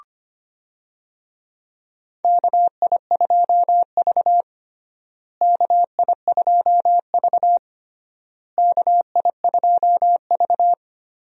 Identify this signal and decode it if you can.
Morse code, heard as a keyed tone, about 25 wpm, 700 Hz. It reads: KI2V KI2V KI2V
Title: USA Call Signs - 4 Characters - Repeated 3x - 25wpm